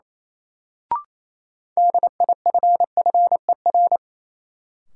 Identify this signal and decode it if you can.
Morse code, heard as a keyed tone, about 28 wpm, 700 Hz. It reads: DIFFER
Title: Sentences from Top 1100 Words - New Word Drill Alt - 28wpm